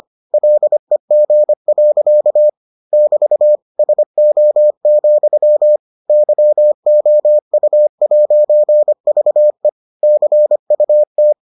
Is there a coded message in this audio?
Yes: Morse code from 0.3 to 11.3 s, heard as a keyed tone, about 25 wpm, 600 Hz. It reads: LEG. <BT> SO, YOU'VE CUT